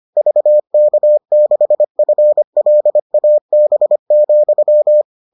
Morse code, heard as a keyed tone, about 25 wpm, 600 Hz. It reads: VK6FLAB,